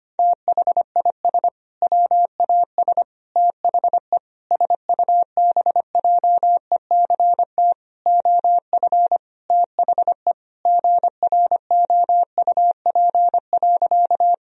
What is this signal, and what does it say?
Morse code, heard as a keyed tone, about 25 wpm, 700 Hz. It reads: THIS WAS THE SUBJECT OF THE GROUP.
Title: Sentences from Top 1000 Words - Review Part 1 - 25wpm